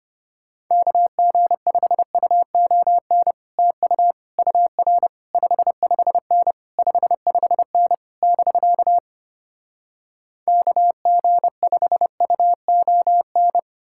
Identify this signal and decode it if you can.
Morse code, heard as a keyed tone, about 33 wpm, 700 Hz. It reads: KG5UON TU UR 55N 55N <BK> KG5UON